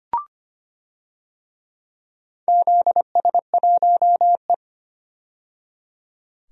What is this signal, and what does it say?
Morse code, heard as a keyed tone, about 25 wpm, 700 Hz. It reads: ZS1E
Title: DXCC-100 Call Signs - 4 Characters - 25wpm